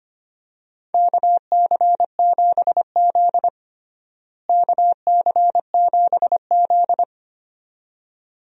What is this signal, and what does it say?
Morse code, heard as a keyed tone, about 25 wpm, 700 Hz. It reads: KC7Z KC7Z
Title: USA Call Signs - 4 Characters - Repeated 2x - 25wpm